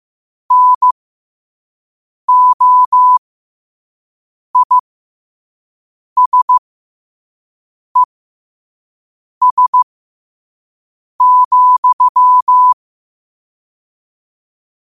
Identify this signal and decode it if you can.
Morse code, heard as a keyed tone, about 15 wpm, spaced as for 5 wpm, 1000 Hz. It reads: NOISES,
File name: qotdfiles / 5wpm